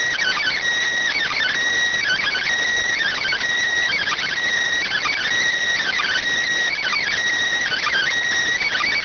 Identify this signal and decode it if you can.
Emergency Alert System (SAME) tones (header bursts), heard as harsh buzzing bursts. It is heard from 0.6 to 8.8 s.